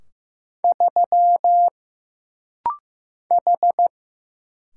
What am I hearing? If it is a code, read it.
Morse code, heard as a keyed tone, about 15 wpm, 700 Hz. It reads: 3 H
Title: Single Letter-Number Alt - Rapid-Fire 15wpm